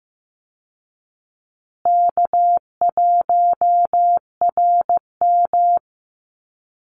Morse code, heard as a keyed tone, about 15 wpm, 700 Hz. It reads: K1RM
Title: Easy Call Signs Repeated 3x - 15wpm